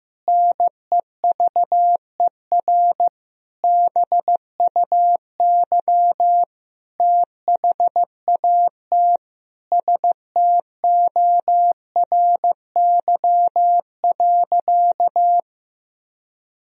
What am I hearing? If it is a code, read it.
Morse code, heard as a keyed tone, about 16 wpm, 700 Hz. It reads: NEVER BUY THAT STORY.
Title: Sentences from Top 1000 Words - Review Part 4 - 15wpm